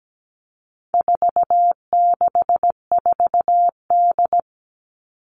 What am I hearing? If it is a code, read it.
Morse code, heard as a keyed tone, about 17 wpm, 700 Hz. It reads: 464D
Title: All Cut Numbers - 4-Digit 17wpm